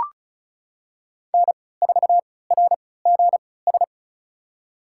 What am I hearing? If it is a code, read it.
Morse code, heard as a keyed tone, about 36 wpm, spaced as for 20 wpm, 700 Hz. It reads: N4RGS
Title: USA Call Signs - 3x Character Spacing - 35wpm